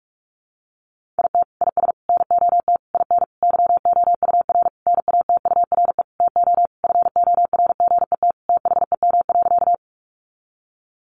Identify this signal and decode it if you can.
Morse code, heard as a keyed tone, about 44 wpm, 700 Hz. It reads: IT IS NOT IN YOUR NATURE TO FORGET THEM.
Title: Sentences from Top 1200 Words 45wpm